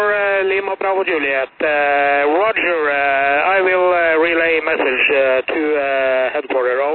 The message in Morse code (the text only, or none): none